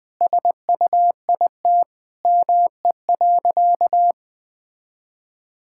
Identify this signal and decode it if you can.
Morse code, heard as a keyed tone, about 20 wpm, 700 Hz. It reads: SUIT ME.
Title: Sentences from Top 1600 Words 20wpm